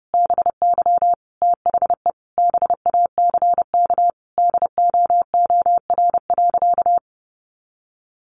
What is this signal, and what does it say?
Morse code, heard as a keyed tone, about 28 wpm, 700 Hz. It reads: BY THE BACK DOOR.
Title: Sentences from Top 300 Words 30wpm